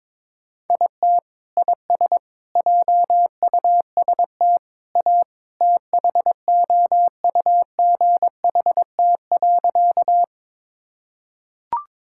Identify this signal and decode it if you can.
Morse code, heard as a keyed tone, about 21 wpm, 700 Hz. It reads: IT IS JUST A THOUGHT.
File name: practice / 22wpm / Sentences with 2-Word Phrases - 22wpm